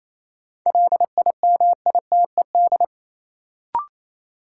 Morse code, heard as a keyed tone, about 28 wpm, 700 Hz. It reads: LIMITED
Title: Sentences from Top 900 Words - New Word Drill - 28wpm